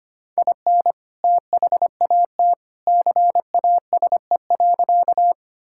Morse code, heard as a keyed tone, about 25 wpm, 700 Hz. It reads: IN THAT CASE.
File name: practice / 25wpm / Sentences with 2-Word Phrases - 25wpm